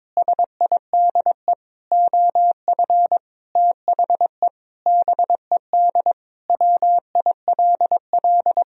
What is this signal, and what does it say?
Morse code, heard as a keyed tone, about 21 wpm, 700 Hz. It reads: SIDE OF THE BED WILL
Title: Sentences from Top 1000 Words - Review Part 1 - 22wpm